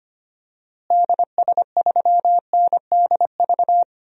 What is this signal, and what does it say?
Morse code, heard as a keyed tone, about 25 wpm, 700 Hz. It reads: DS3NDV
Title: DXCC-100 Call Signs - 6 Characters - Repeated 2x - 25wpm